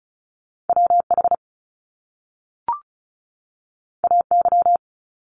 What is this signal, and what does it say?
Morse code, heard as a keyed tone, about 36 wpm, 700 Hz. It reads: WH AY
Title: Top Two Letter Combinations in Words 35wpm